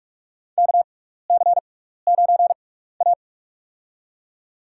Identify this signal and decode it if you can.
Morse code, heard as a keyed tone, about 45 wpm, spaced as for 15 wpm, 700 Hz. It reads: KC9A
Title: USA Call Signs - 6x Character Spacing - 45wpm